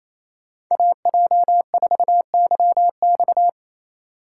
Morse code, heard as a keyed tone, about 37 wpm, 700 Hz. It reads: AJ4YX